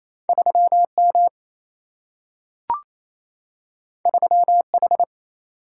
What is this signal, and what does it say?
Morse code, heard as a keyed tone, about 28 wpm, 700 Hz. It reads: 3M 3H
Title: Single-Letter Single-Number 28wpm